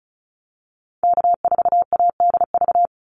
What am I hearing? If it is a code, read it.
Morse code, heard as a keyed tone, about 34 wpm, 700 Hz. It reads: K4ADV
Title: NA SOTA Activator Callsigns 35wpm